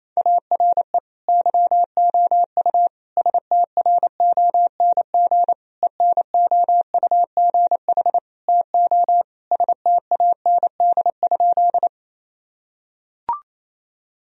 Morse code, heard as a keyed tone, about 28 wpm, 700 Hz. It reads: ARE YOU STRONG ENOUGH TO STAND?
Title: Sentences from Top 1000 Words - Review Part 3 - 28wpm